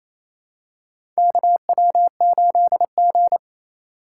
Morse code, heard as a keyed tone, about 28 wpm, 700 Hz. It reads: KW8G
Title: USA Call Signs - 4 Characters - 4x Character Spacing - 28wpm